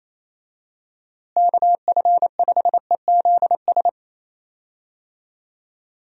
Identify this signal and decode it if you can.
Morse code, heard as a keyed tone, about 28 wpm, 700 Hz. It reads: KF5EZS
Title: USA Call Signs - 6 Characters - Repeated 3x - 28wpm